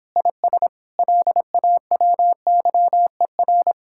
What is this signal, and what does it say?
Morse code, heard as a keyed tone, about 26 wpm, 700 Hz. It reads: IS LAWYER